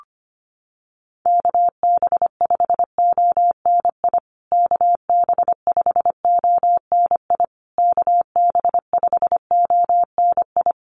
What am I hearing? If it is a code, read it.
Morse code, heard as a keyed tone, about 25 wpm, 700 Hz. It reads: KB5ONI KB5ONI KB5ONI
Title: USA Call Signs Repeated 3x 25wpm